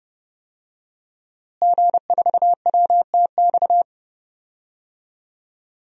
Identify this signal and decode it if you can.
Morse code, heard as a keyed tone, about 30 wpm, 700 Hz. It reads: G4WTX
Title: DXCC-100 Call Signs - 5 Characters - Repeated 3x - 30wpm